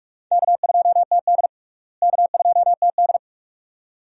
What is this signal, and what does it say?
Morse code, heard as a keyed tone, about 41 wpm, 700 Hz. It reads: K2TD K2TD
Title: USA Call Signs - 4 Characters - Repeated 2x - 45wpm